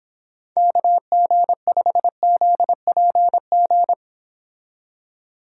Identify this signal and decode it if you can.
Morse code, heard as a keyed tone, about 26 wpm, 700 Hz. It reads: KG5ZPG